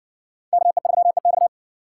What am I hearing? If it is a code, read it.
Morse code, heard as a keyed tone, about 60 wpm, 700 Hz. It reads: KE3EX